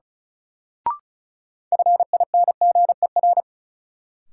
Morse code, heard as a keyed tone, about 36 wpm, 700 Hz. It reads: FINGER
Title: Sentences from Top 1600 Words - New Word Drill - 35wpm